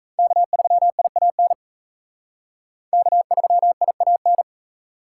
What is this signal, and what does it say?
Morse code, heard as a keyed tone, about 35 wpm, 700 Hz. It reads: K3IAN K3IAN
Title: Speed Racing - USA 1x3 Call Signs 28wpm